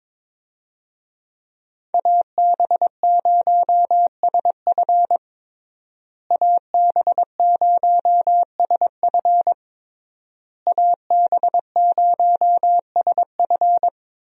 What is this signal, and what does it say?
Morse code, heard as a keyed tone, about 22 wpm, 700 Hz. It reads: AB0SF AB0SF AB0SF
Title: USA Call Signs - 5 Characters - Repeated 3x - 22wpm